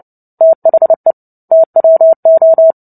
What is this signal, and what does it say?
Morse code, heard as a keyed tone, about 30 wpm, 650 Hz. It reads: THE TWO
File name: 29 WPM